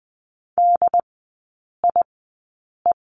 Morse code, heard as a keyed tone, about 20 wpm, spaced as for 8 wpm, 700 Hz. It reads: DIE
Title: 138 Lesson 23e - Letter f - Three Characters 20wpm